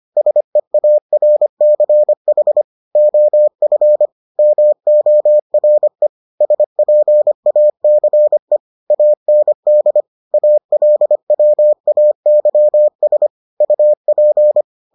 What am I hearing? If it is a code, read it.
Morse code, heard as a keyed tone, about 25 wpm, 600 Hz. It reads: SEARCH OF MORE SPACE AND ALWAYS UP